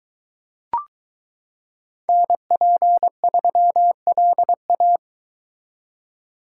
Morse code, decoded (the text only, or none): NP3LA